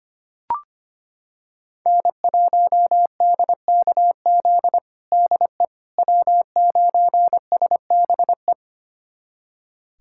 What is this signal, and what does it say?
Morse code, heard as a keyed tone, about 25 wpm, 700 Hz. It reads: N1DKZ DE W9SBE
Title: USA Callsign DE Callsign - 5 Characters - 25wpm